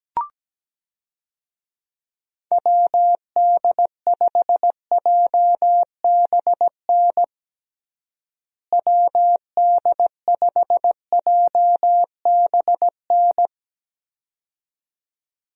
Morse code, heard as a keyed tone, about 17 wpm, 700 Hz. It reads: WD5JBN WD5JBN
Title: USA Call Signs - 6 Characters - Repeated 2x - 17wpm